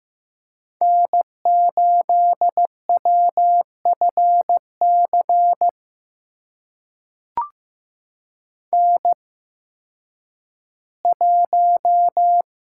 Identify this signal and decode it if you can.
Morse code, heard as a keyed tone, about 15 wpm, spaced as for 10 wpm, 700 Hz. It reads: N8WFC N1
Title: USA Call Signs - 5 Characters - 8x Character Spacing - 15wpm